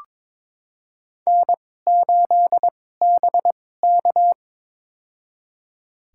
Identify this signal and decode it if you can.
Morse code, heard as a keyed tone, about 21 wpm, spaced as for 16 wpm, 700 Hz. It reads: N8BK
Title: USA Call Signs - 4 Characters - 2x Character Spacing - 22wpm